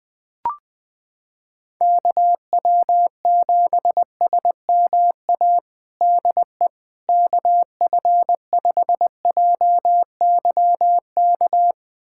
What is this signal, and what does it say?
Morse code, heard as a keyed tone, about 20 wpm, 700 Hz. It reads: KW7SMA DE KF5JYK
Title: USA Callsign DE Callsign 20wpm